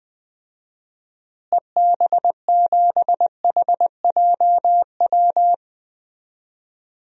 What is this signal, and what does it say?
Morse code, heard as a keyed tone, about 20 wpm, 700 Hz. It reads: EB7HJW
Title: DXCC-100 Call Signs - 6 Characters - Repeated 2x - 20wpm